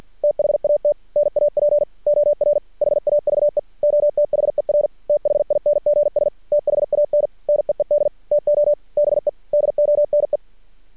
Morse code, heard as a keyed tone, about 53 wpm, 590 Hz. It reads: THAT NAP OR HAVE OTHER THINGS THAN NEED TO BE DONE
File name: archived / CRSnet